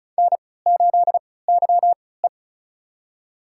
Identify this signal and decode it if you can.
Morse code, heard as a keyed tone, about 36 wpm, spaced as for 20 wpm, 700 Hz. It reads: N8YE
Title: USA Call Signs - 4 Characters - 3x Character Spacing - 35wpm